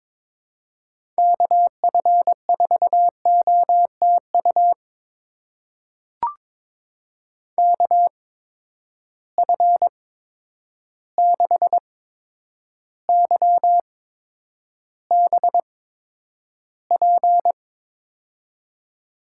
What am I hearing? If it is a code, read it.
Morse code, heard as a keyed tone, about 22 wpm, spaced as for 9 wpm, 700 Hz. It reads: KF4OTU KF6YBP
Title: USA Call Signs - 6 Characters - 8x Character Spacing - 22wpm